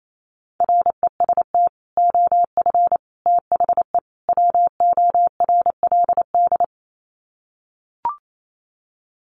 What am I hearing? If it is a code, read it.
Morse code, heard as a keyed tone, about 28 wpm, 700 Hz. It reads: REST OF THE WORLD
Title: Sets of 4 Words 28wpm